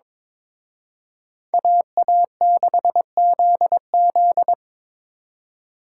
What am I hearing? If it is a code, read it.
Morse code, heard as a keyed tone, about 26 wpm, 700 Hz. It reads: AA6ZZ
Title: Speed Racing - USA 2x2 Call Signs 22wpm